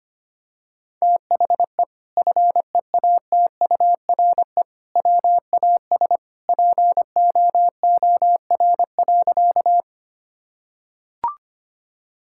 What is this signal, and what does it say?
Morse code, heard as a keyed tone, about 25 wpm, 700 Hz. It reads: THE FEATURE WAS POOR.
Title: Sentences from Top 1000 Words - Review Part 5 - 25wpm